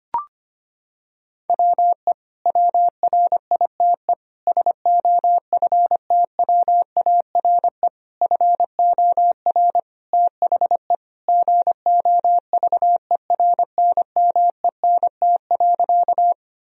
Morse code, heard as a keyed tone, about 25 wpm, 700 Hz. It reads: WE WRITE SOFTWARE FOR THE GOVERNMENT.
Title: Sentences from Top 1000 Words - Review Part 3 - 25wpm